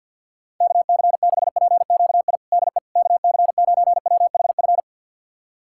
Morse code, heard as a keyed tone, about 52 wpm, 700 Hz. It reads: KC6PQI DE KC9WHF
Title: USA Callsign DE Callsign - 6 Characters - 50wpm